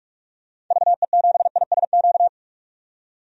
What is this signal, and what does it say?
Morse code, heard as a keyed tone, about 46 wpm, 700 Hz. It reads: VE7ISQ
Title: DXCC-100 Call Signs - 6 Characters - 45wpm